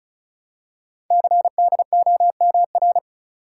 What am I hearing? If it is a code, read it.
Morse code, heard as a keyed tone, about 36 wpm, 690 Hz. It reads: CDOMR
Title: Code Groups - 5 Letters - 35wpm